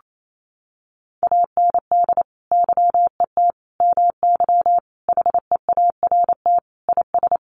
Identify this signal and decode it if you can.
Morse code, heard as a keyed tone, about 28 wpm, 700 Hz. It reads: AND YET MY HEART IS
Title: Sentences from Top 1200 Words 28wpm